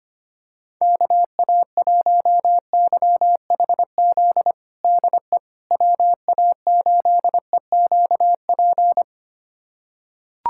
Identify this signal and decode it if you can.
Morse code, heard as a keyed tone, about 25 wpm, 700 Hz. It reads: KA1YHZ DE WA8EQP
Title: USA Callsign DE Callsign - 6 Characters - 25wpm